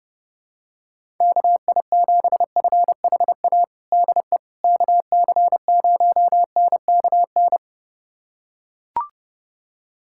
Morse code, heard as a keyed tone, about 32 wpm, 700 Hz. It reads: KI7FHA DE KC0NKN
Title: USA Callsign DE Callsign 30wpm